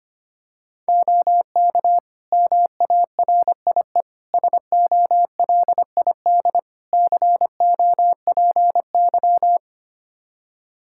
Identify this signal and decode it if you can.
Morse code, heard as a keyed tone, about 25 wpm, 700 Hz. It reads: OK MARIE SOLID COPY
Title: Practice QSO - Exchange 3 - More Info - 25wpm